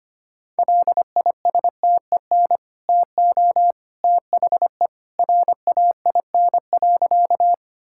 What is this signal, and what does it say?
Morse code, heard as a keyed tone, about 25 wpm, 700 Hz. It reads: LISTEN TO THE RAIN.
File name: Easy Sentences with Hints - 25wpm